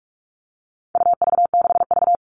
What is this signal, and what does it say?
Morse code, heard as a keyed tone, about 44 wpm, 700 Hz. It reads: UV6V
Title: All Cut Numbers - 4-Digit 45wpm